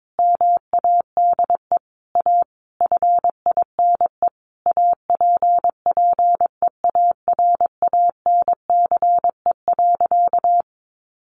Morse code, heard as a keyed tone, about 22 wpm, 700 Hz. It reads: MADE A FINE APPEARANCE.
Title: Sentences from Top 1100 Words 22wpm